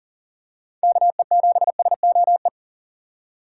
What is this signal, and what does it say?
Morse code, heard as a keyed tone, about 40 wpm, 700 Hz. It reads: KE7SOE
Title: USA Call Signs - 6 Characters - 4x Character Spacing - 40wpm